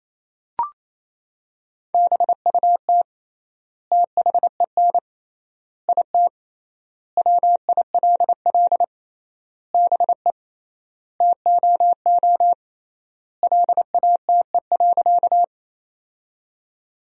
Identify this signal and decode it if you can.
Morse code, heard as a keyed tone, about 27 wpm, 700 Hz. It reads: BUT THEN IT WILL BE TOO LATE.
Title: 3x Word-Spacing - Sentences from Top 300 Words - 28wpm